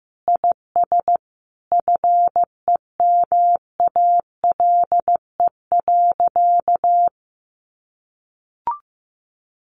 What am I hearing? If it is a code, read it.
Morse code, heard as a keyed tone, about 15 wpm, 700 Hz. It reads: IS FEMALE.
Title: Easy Phrases 15wpm